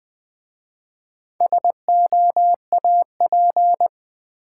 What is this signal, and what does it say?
Morse code, heard as a keyed tone, about 20 wpm, 700 Hz. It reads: SOAP
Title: Easy Words with Hints - 20wpm